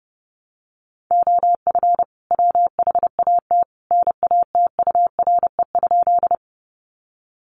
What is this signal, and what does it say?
Morse code, heard as a keyed tone, about 30 wpm, 700 Hz. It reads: OF WHAT NATURE?
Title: Sentences from Top 600 Words 30wpm